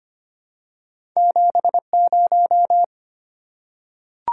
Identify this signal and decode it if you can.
Morse code, heard as a keyed tone, about 25 wpm, 700 Hz. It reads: 70
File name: Two Letter-Number Combinations - 25wpm